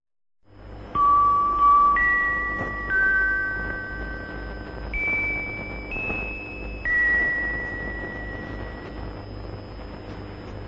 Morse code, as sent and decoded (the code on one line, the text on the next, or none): none